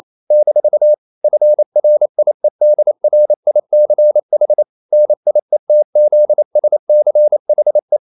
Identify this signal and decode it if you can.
Morse code, heard as a keyed tone, about 28 wpm, 600 Hz. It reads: - FRIEDRICH NIETZSCHE
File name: CW / Practice / EnglishQuotations0128